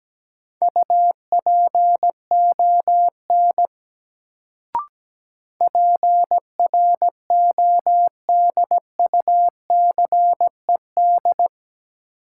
Morse code, heard as a keyed tone, about 17 wpm, 700 Hz. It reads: UPON PRODUCED
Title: Sentences from Top 1000 Words - Review Part 6 - New Word Drill - 17wpm